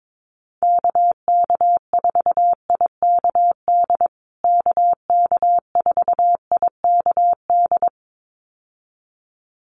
Morse code, heard as a keyed tone, about 22 wpm, 700 Hz. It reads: KK4IKD KK4IKD
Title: USA Call Signs Repeated 2x 22wpm